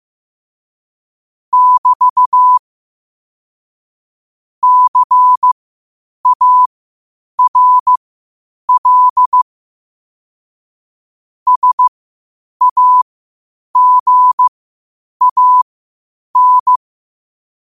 Morse code, heard as a keyed tone, about 15 wpm, spaced as for 8 wpm, 1000 Hz. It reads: <BT> CARL SAGAN